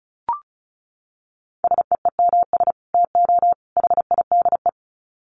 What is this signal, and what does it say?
Morse code, heard as a keyed tone, about 34 wpm, 700 Hz. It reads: SEEMS TO HIDE
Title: Dit Challenge 35wpm